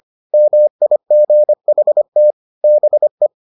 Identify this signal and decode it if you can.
Morse code, heard as a keyed tone, about 25 wpm, 600 Hz. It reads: MIGHT BE